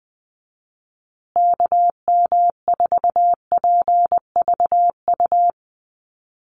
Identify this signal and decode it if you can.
Morse code, heard as a keyed tone, about 20 wpm, 700 Hz. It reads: KM4PVU
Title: USA Call Signs Repeated 3x 20wpm